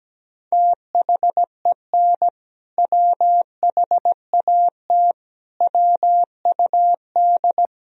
Morse code, heard as a keyed tone, about 17 wpm, 700 Hz. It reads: THEN WHAT WUD
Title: Sentences from Top 400 Words and Abbreviations 17wpm